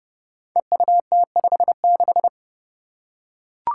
Morse code, heard as a keyed tone, about 30 wpm, 700 Hz. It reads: EUT56
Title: Code Groups - 5 Letter:Numbers - 30wpm